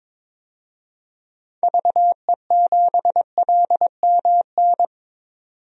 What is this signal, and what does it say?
Morse code, heard as a keyed tone, about 22 wpm, 700 Hz. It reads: VE7LMN
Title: DXCC-100 Call Signs - 6 Characters - 22wpm